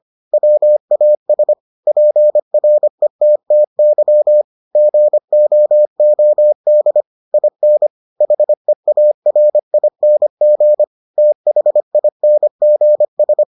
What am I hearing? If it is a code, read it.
Morse code, heard as a keyed tone, about 25 wpm, 600 Hz. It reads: WAS PRETTY GOOD IN HEARING THINGS